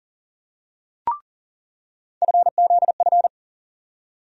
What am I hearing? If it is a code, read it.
Morse code, heard as a keyed tone, about 40 wpm, 700 Hz. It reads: FZF
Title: USA Call Sign Postfixes - 3 Letters 40wpm